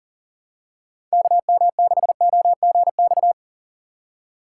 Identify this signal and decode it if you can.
Morse code, heard as a keyed tone, about 39 wpm, 700 Hz. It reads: KM6OGX
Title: USA Call Signs - 6 Characters - 4x Character Spacing - 40wpm